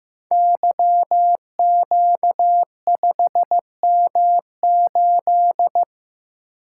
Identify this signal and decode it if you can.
Morse code, heard as a keyed tone, about 15 wpm, 700 Hz. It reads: YQ5M8
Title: Code Groups - 5 Letter:Numbers - 15wpm